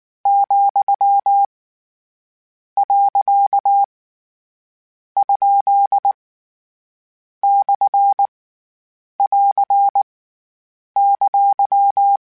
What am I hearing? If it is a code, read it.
Morse code, heard as a keyed tone, about 19 wpm, 800 Hz. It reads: , . ? / <AR> !
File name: Short Runs / Short Runs-mp3s / Words @18